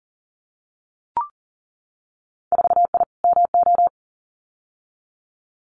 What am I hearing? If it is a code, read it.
Morse code, heard as a keyed tone, about 41 wpm, 700 Hz. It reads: 4I MO
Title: Winter Field Day Exchanges 40wpm